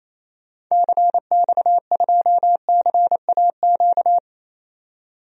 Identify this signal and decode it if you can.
Morse code, heard as a keyed tone, about 28 wpm, 700 Hz. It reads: CX2CAQ
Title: DXCC-100 Call Signs - 6 Characters - 28wpm